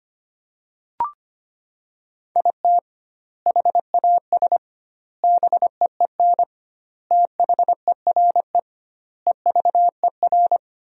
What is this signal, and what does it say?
Morse code, heard as a keyed tone, about 25 wpm, 700 Hz. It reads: IT HAS BEEN THERE EVER
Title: 2x Word-Spacing - Sentences from Top 400 Words - 25wpm